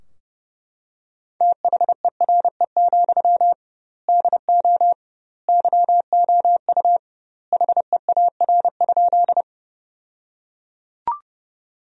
Morse code, heard as a keyed tone, about 30 wpm, 700 Hz. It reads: THERE, DO YOU HEAR?
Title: 2x Word-Spacing - Sentences from Top 300 Words - 30wpm